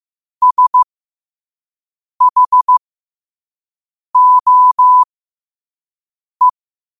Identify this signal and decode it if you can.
Morse code, heard as a keyed tone, about 15 wpm, spaced as for 5 wpm, 1000 Hz. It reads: SHOE
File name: qotdfiles / 5wpm